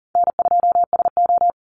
Morse code, heard as a keyed tone, about 40 wpm, 700 Hz. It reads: N2SO
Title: Call Signs - 40wpm